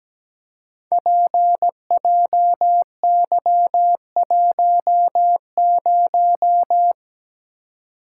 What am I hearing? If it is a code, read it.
Morse code, heard as a keyed tone, about 17 wpm, 700 Hz. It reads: PJY10
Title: Code Groups - 5 Letter:Numbers - 17wpm